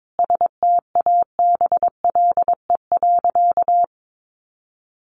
Morse code, heard as a keyed tone, about 22 wpm, 700 Hz. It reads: STABLE.
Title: Sentences from Top 1500 Words 22wpm